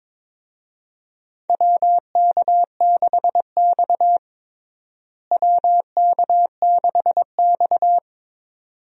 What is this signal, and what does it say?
Morse code, heard as a keyed tone, about 22 wpm, 700 Hz. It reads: WK6X WK6X
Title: USA Call Signs - 4 Characters - Repeated 2x - 22wpm